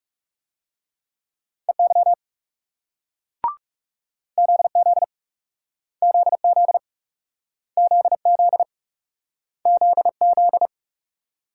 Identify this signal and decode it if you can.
Morse code, heard as a keyed tone, about 41 wpm, 700 Hz. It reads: EY ZZ ZZ ZZ ZZ